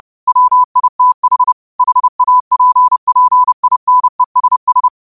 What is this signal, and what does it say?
Morse code, heard as a keyed tone, about 30 wpm, 1000 Hz. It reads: WITH HAPPINESS
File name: qotdfiles / 30wpm